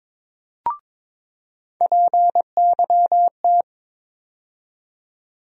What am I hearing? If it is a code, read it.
Morse code, heard as a keyed tone, about 22 wpm, 700 Hz. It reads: PYT
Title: USA Call Sign Postfixes - 3 Letters 22wpm